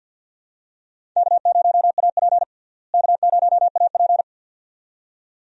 Morse code, heard as a keyed tone, about 50 wpm, 700 Hz. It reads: K0AP K0AP
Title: USA Call Signs - 4 Characters - Repeated 2x - 50wpm